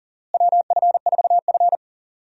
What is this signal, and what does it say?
Morse code, heard as a keyed tone, about 40 wpm, 700 Hz. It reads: WF4F
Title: USA Call Signs - 4 Characters - Repeated 3x - 40wpm